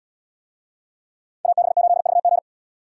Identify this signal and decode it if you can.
Morse code, heard as a keyed tone, about 70 wpm, 700 Hz. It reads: AB8VG